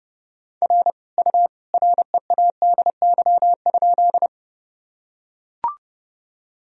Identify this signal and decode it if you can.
Morse code, heard as a keyed tone, about 32 wpm, 700 Hz. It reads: R U READY?
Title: Sentences from Top 400 Words and Abbreviations 30wpm